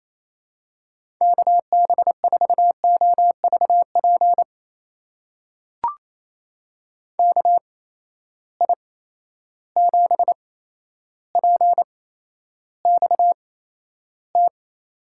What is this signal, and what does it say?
Morse code, heard as a keyed tone, about 28 wpm, spaced as for 12 wpm, 700 Hz. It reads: KB4OVP KI7PXT
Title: USA Call Signs - 6 Characters - 8x Character Spacing - 28wpm